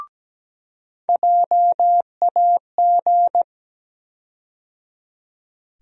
Morse code, heard as a keyed tone, about 17 wpm, 700 Hz. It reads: JAG
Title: USA Call Sign Postfixes - 3 Letters 17wpm